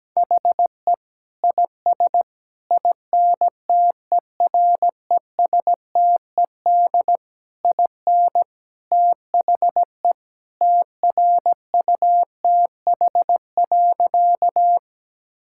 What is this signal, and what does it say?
Morse code, heard as a keyed tone, about 17 wpm, 700 Hz. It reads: HE IS INTERESTED IN THE TRUTH.
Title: Sentences from Top 1000 Words - Review Part 6 - 17wpm